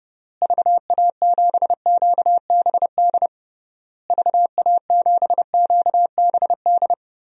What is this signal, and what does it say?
Morse code, heard as a keyed tone, about 30 wpm, 700 Hz. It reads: VA7QBD VA7QBD
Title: DXCC-100 Call Signs - 6 Characters - Repeated 3x - 30wpm